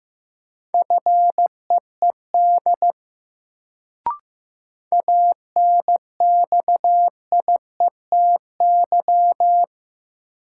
Morse code, heard as a keyed tone, about 15 wpm, 700 Hz. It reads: FEED ANXIETY
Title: Sentences from Top 1600 Words - New Word Drill Alt - 15wpm